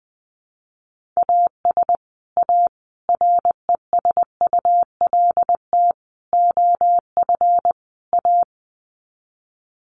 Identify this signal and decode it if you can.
Morse code, heard as a keyed tone, about 20 wpm, 700 Hz. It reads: AS A RESULT OF A
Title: Sets of 5 Words Encore 20wpm